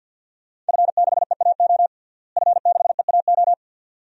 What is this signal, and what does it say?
Morse code, heard as a keyed tone, about 46 wpm, 700 Hz. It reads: F6EAO F6EAO
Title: DXCC-100 Call Signs - 5 Characters - Repeated 2x - 50wpm